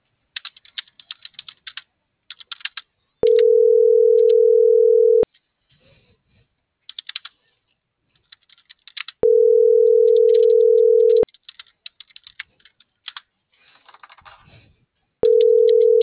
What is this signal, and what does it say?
Ringback tone (North America), 440 + 480 Hz.